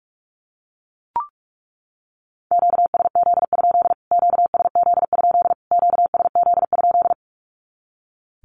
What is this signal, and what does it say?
Morse code, heard as a keyed tone, about 44 wpm, 700 Hz. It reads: QSZ? QSZ? QSZ?
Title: Q-Codes Repeated 3x 45wpm